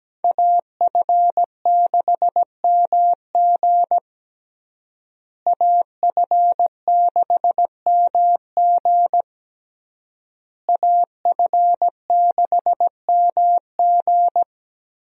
Morse code, heard as a keyed tone, about 17 wpm, 700 Hz. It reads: AF6MG AF6MG AF6MG
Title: USA Call Signs - 5 Characters - Repeated 3x - 17wpm